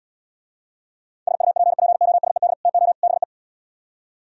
Morse code, heard as a keyed tone, about 76 wpm, 700 Hz. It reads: SUPPOSED TO BE